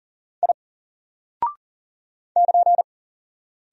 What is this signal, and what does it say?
Morse code, heard as a keyed tone, about 39 wpm, 700 Hz. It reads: I <KN>
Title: Single Character (Letter:Number:Punctuation:Prosign) 40wpm